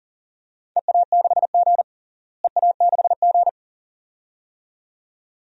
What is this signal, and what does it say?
Morse code, heard as a keyed tone, about 40 wpm, 690 Hz. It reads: EA6G EA6G
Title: DXCC-100 Call Signs - 4 Characters - Repeated 2x - 40wpm